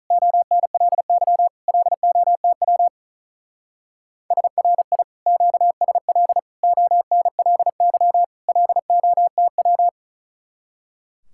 Morse code, heard as a keyed tone, about 43 wpm, 700 Hz. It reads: ONLY LOTW SRI QSL ONLY LOTW
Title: Speed-Racing Common QSO Phrases - 35wpm